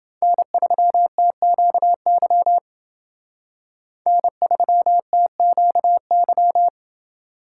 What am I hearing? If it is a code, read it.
Morse code, heard as a keyed tone, about 28 wpm, 700 Hz. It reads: N3TQY N3TQY